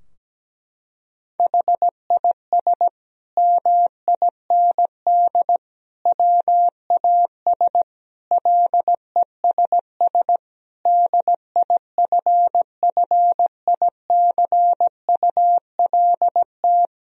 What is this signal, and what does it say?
Morse code, heard as a keyed tone, about 17 wpm, 700 Hz. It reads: HIS MIND WAS LESS DIFFICULT
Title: Sentences from Top 1000 Words - Review Part 2 - 17wpm